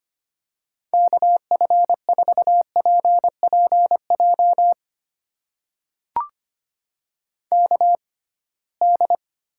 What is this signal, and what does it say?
Morse code, heard as a keyed tone, about 25 wpm, spaced as for 19 wpm, 700 Hz. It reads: KF4PPJ KD
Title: USA Call Signs - 6x Character Spacing - 25wpm